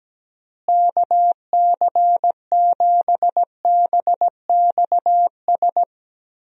Morse code, heard as a keyed tone, about 17 wpm, 700 Hz. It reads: KC7BXS
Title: USA Call Signs - 6 Characters - Repeated 3x - 17wpm